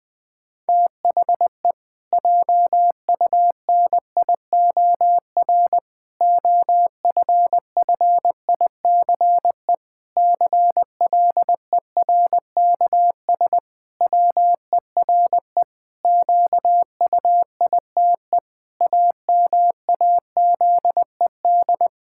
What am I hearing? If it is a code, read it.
Morse code, heard as a keyed tone, about 20 wpm, 700 Hz. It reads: THE JUNIOR OFFICE CLERKS WERE QUITE AMAZED
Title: Pangram Challenge v2 - 20wpm